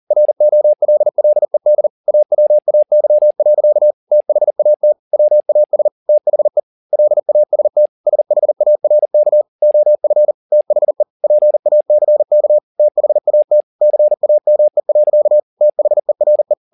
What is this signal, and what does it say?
Morse code, heard as a keyed tone, about 40 wpm, 600 Hz. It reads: ROLLED AWAY. THAT WAS THE LAST SHARK OF THE PACK THAT CAME. THERE
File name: the old man and the sea 2 40wpm